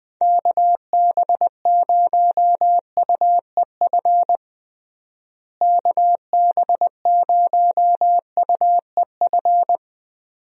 Morse code, heard as a keyed tone, about 20 wpm, 700 Hz. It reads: KB0UEF KB0UEF
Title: USA Call Signs - 6 Characters - Repeated 2x - 20wpm